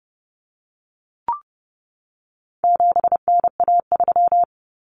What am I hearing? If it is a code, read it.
Morse code, heard as a keyed tone, about 30 wpm, 700 Hz. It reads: 7NA3
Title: Common Cut Numbers - 4-Digit 30wpm